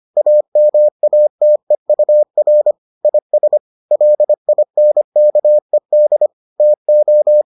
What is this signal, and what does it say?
Morse code, heard as a keyed tone, about 25 wpm, 600 Hz. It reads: AMATEUR IS LINKED TO